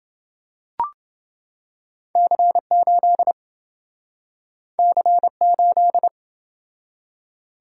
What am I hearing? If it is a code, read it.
Morse code, heard as a keyed tone, about 27 wpm, 700 Hz. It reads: C8 C8